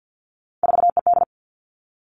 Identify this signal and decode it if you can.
Morse code, heard as a keyed tone, about 50 wpm, 700 Hz. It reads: 4ED
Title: Three Challenging Letter-Number Combinations 50wpm